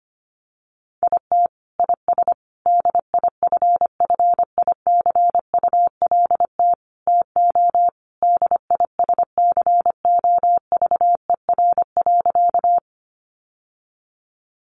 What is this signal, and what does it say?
Morse code, heard as a keyed tone, about 25 wpm, 700 Hz. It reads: IT IS DIFFICULT TO DISCOVER.
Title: Sentences from Top 1100 Words 25wpm